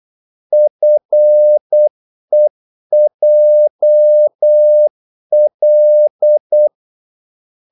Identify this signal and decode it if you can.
Morse code, heard as a keyed tone, about 8 wpm, 600 Hz. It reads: FEJL
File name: Korte ord DK 8wpm